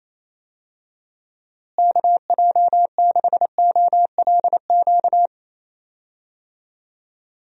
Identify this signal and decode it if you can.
Morse code, heard as a keyed tone, about 28 wpm, 700 Hz. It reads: KJ6OLQ
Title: USA Call Signs - 6 Characters - Repeated 2x - 28wpm